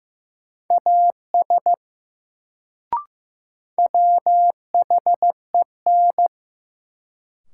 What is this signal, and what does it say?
Morse code, heard as a keyed tone, about 15 wpm, 700 Hz. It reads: AS WHEN
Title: Sentences from Top 100 Words - New Word Drill - 15wpm